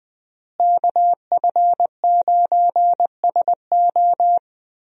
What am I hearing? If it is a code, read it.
Morse code, heard as a keyed tone, about 21 wpm, 700 Hz. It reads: KF9SO
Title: USA Call Signs - 5 Characters - Repeated 3x - 20wpm